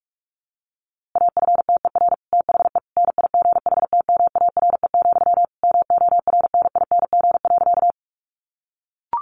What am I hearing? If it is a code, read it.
Morse code, heard as a keyed tone, about 44 wpm, 700 Hz. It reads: AFTER THE NIGHTMARE, MORNING.
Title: A Sound Of Thunder 45wpm